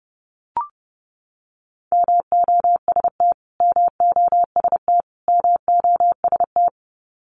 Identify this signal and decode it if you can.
Morse code, heard as a keyed tone, about 30 wpm, 700 Hz. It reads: MOST MOST MOST
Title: Top 500 Words Repeated 30wpm